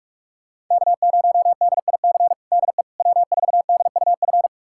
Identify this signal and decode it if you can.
Morse code, heard as a keyed tone, about 45 wpm, 700 Hz. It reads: K0DIC DE W4DUF
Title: USA Callsign DE Callsign - 5 Characters - 45wpm